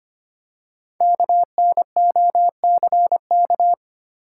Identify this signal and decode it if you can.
Morse code, heard as a keyed tone, about 25 wpm, 700 Hz. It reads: KNOCK
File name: 5 Letter Words - First 3 Letters Given - 25wpm